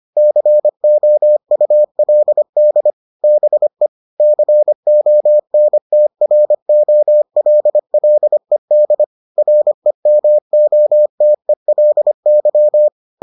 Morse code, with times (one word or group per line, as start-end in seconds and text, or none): 0.2-2.9 COULD
3.2-3.9 BE
4.2-9.0 CONTROLLED
9.4-12.9 REMOTELY